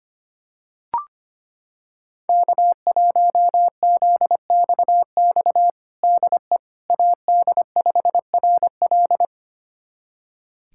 Morse code, heard as a keyed tone, about 25 wpm, 700 Hz. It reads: K1ZXX DE AD5RL
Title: USA Callsign DE Callsign - 5 Characters - 25wpm